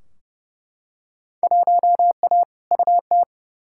Morse code, heard as a keyed tone, about 31 wpm, 700 Hz. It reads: 1A UT
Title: Field Day Exchanges - 30wpm